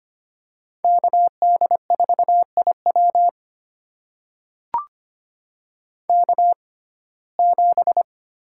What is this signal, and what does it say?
Morse code, heard as a keyed tone, about 25 wpm, spaced as for 18 wpm, 700 Hz. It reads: KD4IW K7
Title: USA Call Signs - 5 Characters - 6x Character Spacing - 25wpm